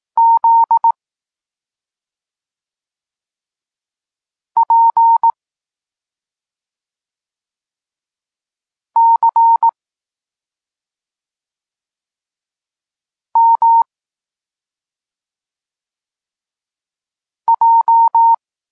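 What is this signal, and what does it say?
Morse code, heard as a keyed tone, about 18 wpm, 920 Hz. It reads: Z P C M J
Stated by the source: Polish Morse lesson teaching the letter C, from SP9MOA Niepołomicki Klub Krótkofalowców.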